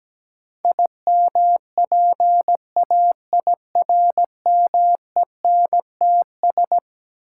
Morse code, heard as a keyed tone, about 17 wpm, 700 Hz. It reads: IMPAIRMENTS
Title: Sets of 5 Words Encore - New Word Drill - 17wpm